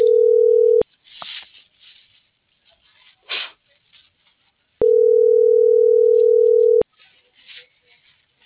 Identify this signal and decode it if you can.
Ringback tone (North America), 440 + 480 Hz.